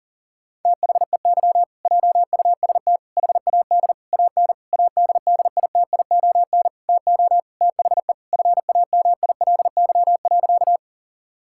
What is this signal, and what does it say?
Morse code, heard as a keyed tone, about 40 wpm, 700 Hz. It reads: THEY JUST HAD AN ADDITION TO THE FAMILY.
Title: Sentences from Top 1000 Words - Review Part 6 - 40wpm